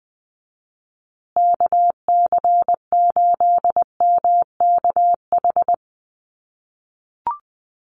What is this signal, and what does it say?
Morse code, heard as a keyed tone, about 20 wpm, 700 Hz. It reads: KC8MKH
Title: USA Call Signs Repeated 2x 20wpm